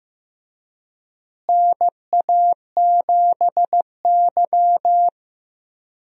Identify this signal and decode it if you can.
Morse code, heard as a keyed tone, about 15 wpm, 700 Hz. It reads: NA7Y
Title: USA Call Signs - 4 Characters - Repeated 3x - 15wpm